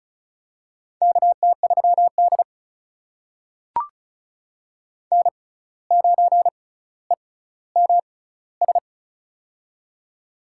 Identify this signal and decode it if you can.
Morse code, heard as a keyed tone, about 36 wpm, spaced as for 17 wpm, 700 Hz. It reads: KT3D N9EMS
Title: USA Call Signs - 6x Character Spacing - 35wpm